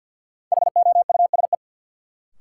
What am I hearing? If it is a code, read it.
Morse code, heard as a keyed tone, about 51 wpm, 700 Hz. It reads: HOUSE
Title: Sentences from Top 200 Words - New Word Drill - 50wpm